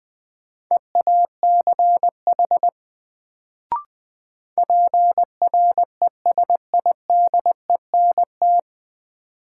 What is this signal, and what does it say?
Morse code, heard as a keyed tone, about 20 wpm, 700 Hz. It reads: EACH PRESIDENT
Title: Sets of 2 Words - New Word Drill - 20wpm